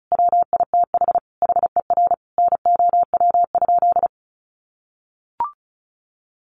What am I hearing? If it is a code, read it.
Morse code, heard as a keyed tone, about 34 wpm, 700 Hz. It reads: WITH HER NOW?
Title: Sentences from Top 900 Words 35wpm